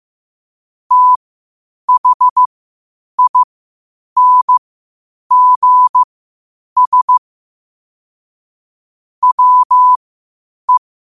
Morse code, heard as a keyed tone, about 15 wpm, spaced as for 8 wpm, 1000 Hz. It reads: THINGS WE
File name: qotdfiles / 8wpm